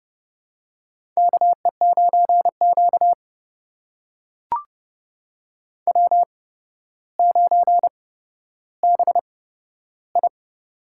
Morse code, heard as a keyed tone, about 30 wpm, spaced as for 13 wpm, 700 Hz. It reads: KE9Q W9BI
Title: USA Call Signs - 4 Characters - 8x Character Spacing - 30wpm